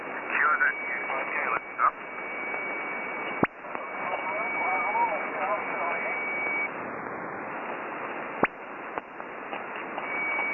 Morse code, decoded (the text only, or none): none